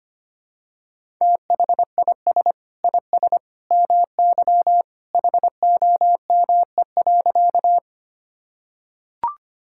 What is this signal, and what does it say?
Morse code, heard as a keyed tone, about 25 wpm, 700 Hz. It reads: THIS IS MY HOME.
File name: practice / 25wpm / Sentences with 2-Word Phrases - 25wpm